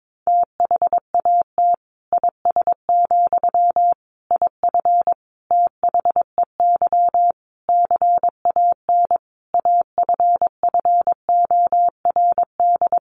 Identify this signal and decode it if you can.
Morse code, heard as a keyed tone, about 22 wpm, 700 Hz. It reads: THAT IS, IF THEY CAN AFFORD
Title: Sentences from Top 1100 Words 22wpm